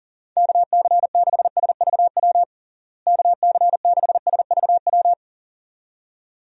Morse code, heard as a keyed tone, about 40 wpm, 700 Hz. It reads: KC6SVW KC6SVW
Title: USA Call Signs - 6 Characters - Repeated 2x - 40wpm